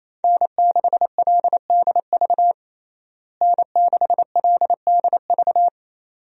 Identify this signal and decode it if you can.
Morse code, heard as a keyed tone, about 27 wpm, 700 Hz. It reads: N6LDV N6LDV
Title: USA Call Signs - 5 Characters - Repeated 2x - 28wpm